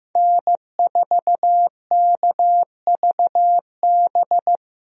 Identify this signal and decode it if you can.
Morse code, heard as a keyed tone, about 15 wpm, 700 Hz. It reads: N4KVB
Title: USA Call Signs - 5 Characters - Repeated 3x - 15wpm